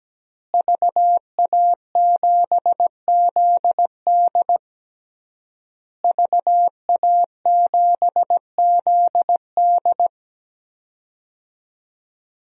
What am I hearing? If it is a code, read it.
Morse code, heard as a keyed tone, about 17 wpm, 700 Hz. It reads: VA7ZD VA7ZD
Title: DXCC-100 Call Signs - 5 Characters - Repeated 3x - 17wpm